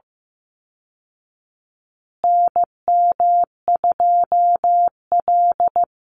Morse code, heard as a keyed tone, about 15 wpm, 700 Hz. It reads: NM2L
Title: Call Signs Repeated 3x - 15wpm